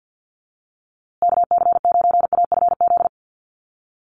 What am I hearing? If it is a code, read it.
Morse code, heard as a keyed tone, about 50 wpm, 700 Hz. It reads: KC9AFZ
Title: USA Call Signs Repeated 3x 50wpm